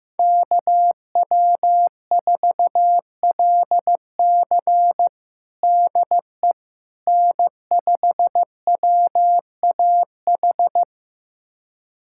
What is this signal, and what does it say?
Morse code, heard as a keyed tone, about 15 wpm, 700 Hz. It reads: KW4LC DE N5WAH
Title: USA Callsign DE Callsign - 5 Characters - 15wpm